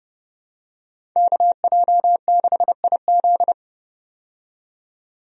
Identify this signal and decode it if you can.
Morse code, heard as a keyed tone, about 30 wpm, 700 Hz. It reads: KJ6IZ
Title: USA Call Signs - 5 Characters - 30wpm